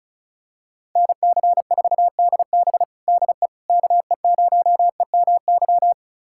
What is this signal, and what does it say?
Morse code, heard as a keyed tone, about 36 wpm, 700 Hz. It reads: NC4DB DE KE0EMY
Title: USA Callsign DE Callsign 35wpm